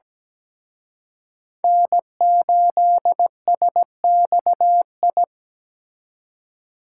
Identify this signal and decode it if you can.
Morse code, heard as a keyed tone, about 20 wpm, 700 Hz. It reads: N8SXI
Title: Speed Racing - USA Call Signs 17wpm